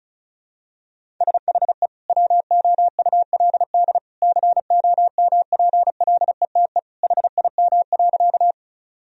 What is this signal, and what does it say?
Morse code, heard as a keyed tone, about 36 wpm, 700 Hz. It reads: SHE WOULD COMPLETE HIM.
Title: Sentences from Top 1000 Words - Review Part 2 - 35wpm